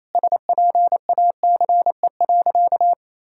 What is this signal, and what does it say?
Morse code, heard as a keyed tone, about 27 wpm, 700 Hz. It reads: SPACE.
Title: Sentences from Top 1000 Words - Review Part 5 - 28wpm